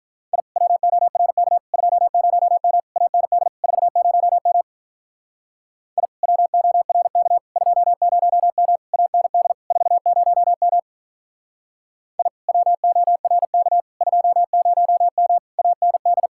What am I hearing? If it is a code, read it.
Morse code, heard as a keyed tone, about 44 wpm, 700 Hz. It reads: I WORK 20M AND 40M I WORK 20M AND 40M I WORK 20M AND